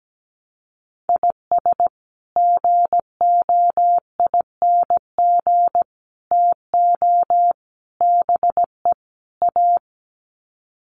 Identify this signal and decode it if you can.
Morse code, heard as a keyed tone, about 17 wpm, 700 Hz. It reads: IS GOING TO BE A
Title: Sets of 5 Words 17wpm